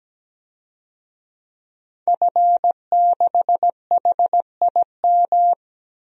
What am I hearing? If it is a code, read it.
Morse code, heard as a keyed tone, about 17 wpm, 700 Hz. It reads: F6HIM
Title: DXCC-100 Call Signs - 5 Characters - Repeated 2x - 17wpm